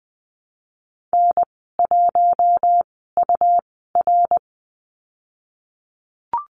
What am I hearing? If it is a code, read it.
Morse code, heard as a keyed tone, about 20 wpm, spaced as for 15 wpm, 700 Hz. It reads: N1UR
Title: 122 Lesson 20g - Number 3 - Call Signs 20wpm